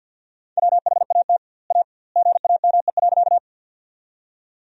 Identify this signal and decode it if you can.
Morse code, heard as a keyed tone, about 53 wpm, 700 Hz. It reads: WHAT A GAME.
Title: 2x Word-Spacing - Sentences from Top 500 Words - 50wpm